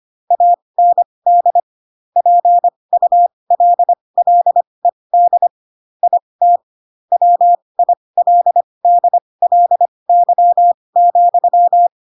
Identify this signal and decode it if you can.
Morse code, heard as a keyed tone, about 25 wpm, spaced as for 20 wpm, 700 Hz. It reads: AND PULLED IT WILDLY,
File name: Pinocchio / Pinocchio ch6 15 25 ik1hge